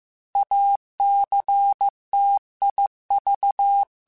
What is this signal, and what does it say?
Morse code, heard as a keyed tone, about 15 wpm, 800 Hz. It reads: ACTIV
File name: Technical Subjects / Technical Subjects-mp3s / OR Gate @13